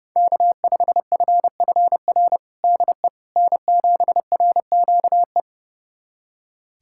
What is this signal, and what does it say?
Morse code, heard as a keyed tone, about 30 wpm, 700 Hz. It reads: K5FFR DE N7RQE
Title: USA Callsign DE Callsign - 5 Characters - 30wpm